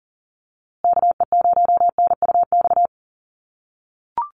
Morse code, heard as a keyed tone, about 40 wpm, 700 Hz. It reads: KE0NUX
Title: USA Call Signs Repeated 3x 40wpm